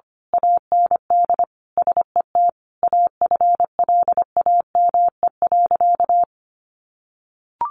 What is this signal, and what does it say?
Morse code, heard as a keyed tone, about 25 wpm, 700 Hz. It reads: AND SET AFLAME.
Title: A Sound Of Thunder 25wpm